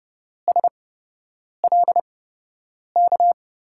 Morse code, heard as a keyed tone, about 30 wpm, spaced as for 8 wpm, 700 Hz. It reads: SLK
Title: USA Call Signs - 6 Characters - 8x Character Spacing - 30wpm